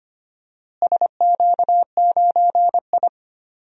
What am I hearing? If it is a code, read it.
Morse code, heard as a keyed tone, about 25 wpm, 700 Hz. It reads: SQ9I
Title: DXCC-100 Call Signs - 4 Characters - Repeated 2x - 25wpm